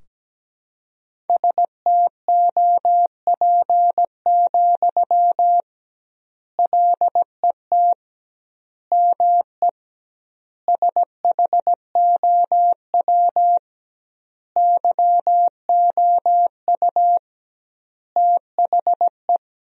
Morse code, heard as a keyed tone, about 17 wpm, 700 Hz. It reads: STOP, LET ME SHOW YOU THE
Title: 2x Word-Spacing - Sentences from Top 300 Words - 17wpm